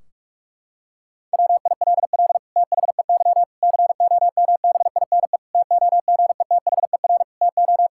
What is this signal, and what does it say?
Morse code, heard as a keyed tone, about 43 wpm, 700 Hz. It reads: WILL THEY COMBINE TOGETHER TO
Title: Sentences from Top 1600 Words 45wpm